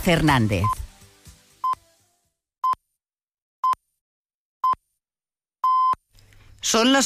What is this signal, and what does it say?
Radio time signal (time pips), with speech. It is heard from 0.6 to 5.9 s.